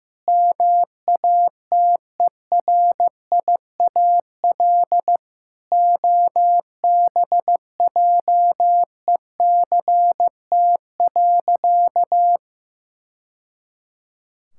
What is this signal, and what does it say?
Morse code, heard as a keyed tone, about 15 wpm, 700 Hz. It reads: MATERIAL OBJECT.
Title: Sentences from Top 1000 Words - Review Part 5 - 15wpm